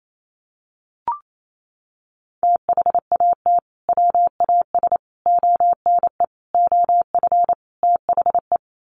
Morse code, heard as a keyed tone, about 28 wpm, 700 Hz. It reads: THAT WAS ONE OF THE
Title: Sets of 5 Words 28wpm